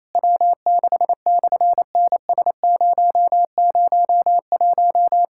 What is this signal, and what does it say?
Morse code, heard as a keyed tone, about 27 wpm, 700 Hz. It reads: W6/NS001
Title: SOTA QSO Practice 28wpm